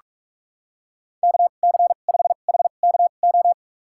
Morse code, heard as a keyed tone, about 41 wpm, spaced as for 31 wpm, 700 Hz. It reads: KC5HKO
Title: USA Call Signs - 2x Character Spacing - 45wpm